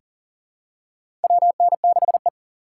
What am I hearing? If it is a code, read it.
Morse code, heard as a keyed tone, about 40 wpm, 700 Hz. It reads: WN6E
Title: USA Call Signs - 4 Characters - 3x Character Spacing - 40wpm